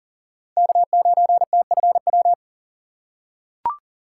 Morse code, heard as a keyed tone, about 40 wpm, 700 Hz. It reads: K9TFW
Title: USA Call Signs - 5 Characters - 3x Character Spacing - 40wpm